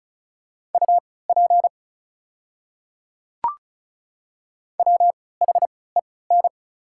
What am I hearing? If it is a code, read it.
Morse code, heard as a keyed tone, about 36 wpm, spaced as for 20 wpm, 690 Hz. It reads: UP WHEN
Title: 3x Character Spacing - Top 100 Words - 35wpm